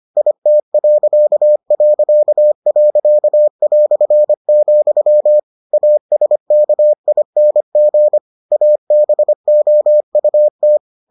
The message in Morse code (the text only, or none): IT...", ASKING ABOUT